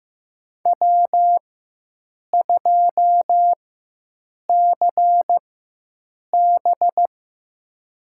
Morse code, heard as a keyed tone, about 15 wpm, spaced as for 7 wpm, 700 Hz. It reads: W2CB
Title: USA Call Signs - 4 Characters - 4x Character Spacing - 15wpm